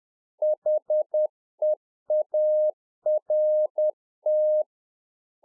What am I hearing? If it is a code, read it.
Morse code, heard as a keyed tone, about 10 wpm, 610 Hz. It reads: HEART
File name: UNFOGETTABLE QSO 10WPM